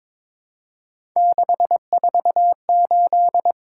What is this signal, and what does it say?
Morse code, heard as a keyed tone, about 22 wpm, 700 Hz. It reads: 648
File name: practice / 22wpm / Three Letter-Number Combinations - 22wpm